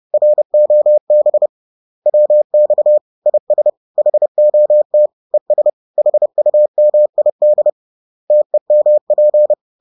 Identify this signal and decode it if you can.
Morse code, heard as a keyed tone, about 30 wpm, 600 Hz. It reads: ROB WX IS HOT ES HUMID TEMP